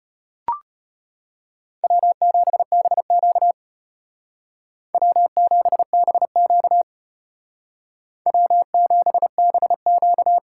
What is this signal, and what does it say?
Morse code, heard as a keyed tone, about 33 wpm, 700 Hz. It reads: W7BQ W7BQ W7BQ